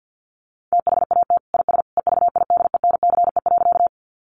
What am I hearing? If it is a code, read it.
Morse code, heard as a keyed tone, about 50 wpm, 700 Hz. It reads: THAT IS EVIDENCE.
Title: Sentences from Top 1300 Words 50wpm